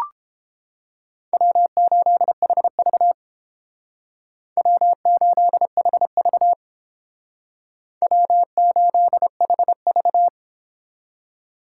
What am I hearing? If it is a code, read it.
Morse code, heard as a keyed tone, about 29 wpm, 700 Hz. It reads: W8HV W8HV W8HV